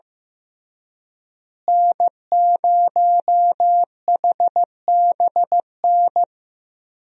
Morse code, heard as a keyed tone, about 16 wpm, 700 Hz. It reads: N0HBN
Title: USA Call Signs - 5 Characters - Repeated 3x - 15wpm